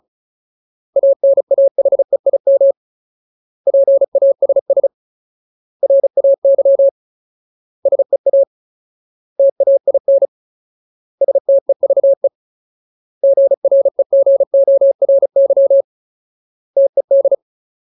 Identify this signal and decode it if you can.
Morse code, heard as a keyed tone, about 34 wpm, 550 Hz. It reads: ANAHEIM PASS RAY SEA TAIN STEVE GREGORY TED
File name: Ultimate Word List / 35WPM Ultimate